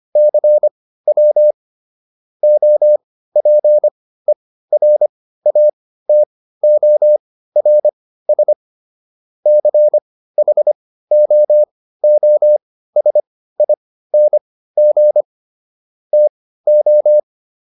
Morse code, heard as a keyed tone, about 25 wpm, spaced as for 15 wpm, 600 Hz. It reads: CW OPERATORS CHOOSING TO